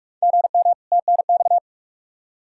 Morse code, heard as a keyed tone, about 43 wpm, 700 Hz. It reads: GM TNX
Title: Practice QSO - Exchange 1 - Beginning - 45wpm